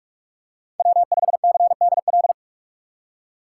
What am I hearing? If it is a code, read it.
Morse code, heard as a keyed tone, about 46 wpm, 700 Hz. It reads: W5CDL
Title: USA Call Signs - 5 Characters - 4x Character Spacing - 45wpm